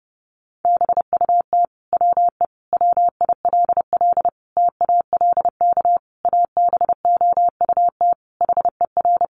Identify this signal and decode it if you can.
Morse code, heard as a keyed tone, about 28 wpm, 700 Hz. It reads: BUT WE WILL TALK ABOUT HER
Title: Sentences from Top 1200 Words 30wpm